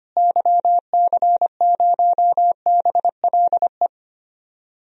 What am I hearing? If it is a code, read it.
Morse code, heard as a keyed tone, about 25 wpm, 700 Hz. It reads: YC0BLE
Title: DXCC-100 Call Signs - 6 Characters - Repeated 3x - 25wpm